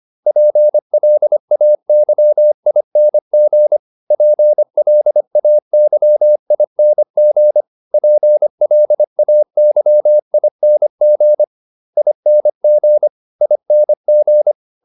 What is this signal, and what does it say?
Morse code, heard as a keyed tone, about 25 wpm, 600 Hz. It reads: PLAYING PLAYING PLAYING ING ING